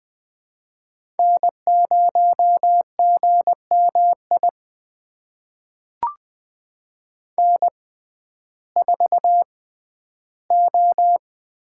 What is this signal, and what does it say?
Morse code, heard as a keyed tone, about 21 wpm, spaced as for 12 wpm, 700 Hz. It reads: N0GMI N4O
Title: USA Call Signs - 5 Characters - 6x Character Spacing - 20wpm